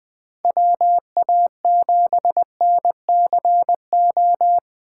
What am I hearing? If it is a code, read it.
Morse code, heard as a keyed tone, about 19 wpm, 700 Hz. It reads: WA7NCO
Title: Speed Racing - USA Call Signs 17wpm